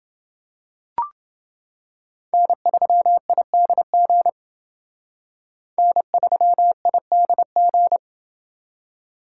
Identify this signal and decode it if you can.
Morse code, heard as a keyed tone, about 29 wpm, 700 Hz. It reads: N3IDG N3IDG